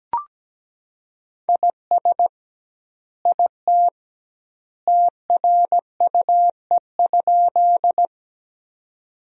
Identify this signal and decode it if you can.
Morse code, heard as a keyed tone, about 17 wpm, 700 Hz. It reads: IS IT TRUE?
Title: 2x Word-Spacing - Sentences from Top 400 Words - 17wpm